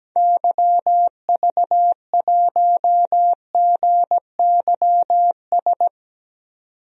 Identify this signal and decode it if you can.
Morse code, heard as a keyed tone, about 17 wpm, 700 Hz. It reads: YV1GYS
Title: DXCC-100 Call Signs - 6 Characters - 17wpm